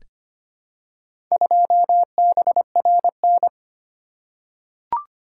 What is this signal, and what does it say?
Morse code, heard as a keyed tone, about 25 wpm, 700 Hz. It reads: 2BRN
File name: practice / 25wpm / Four Letter-Number Combinations - 25wpm